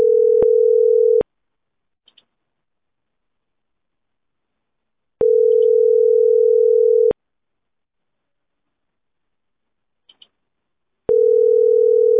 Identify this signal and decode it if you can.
Ringback tone (North America), 440 + 480 Hz.